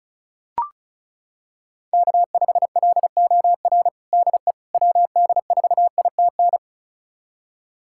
Morse code, heard as a keyed tone, about 34 wpm, 700 Hz. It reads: K5LOR DE WD4ITN
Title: USA Callsign DE Callsign 35wpm